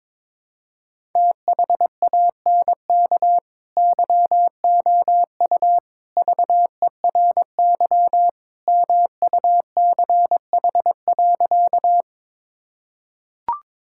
Morse code, heard as a keyed tone, about 21 wpm, 700 Hz. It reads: THANK YOU VERY MUCH.
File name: practice / 22wpm / Sentences with 3-Word Phrases - 22wpm